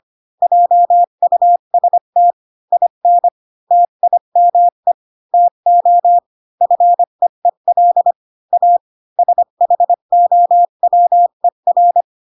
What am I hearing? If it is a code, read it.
Morse code, heard as a keyed tone, about 25 wpm, 700 Hz. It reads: JUST IN TIME TO FEEL A SHOWER